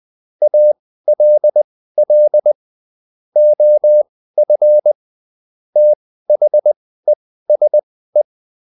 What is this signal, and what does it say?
Morse code, heard as a keyed tone, about 20 wpm, spaced as for 14 wpm, 600 Hz. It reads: ALL OF THESE